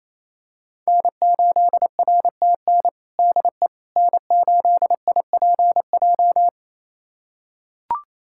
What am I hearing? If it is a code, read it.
Morse code, heard as a keyed tone, about 28 wpm, 700 Hz. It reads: N8RTN DE N8IPJ
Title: USA Callsign DE Callsign - 5 Characters - 28wpm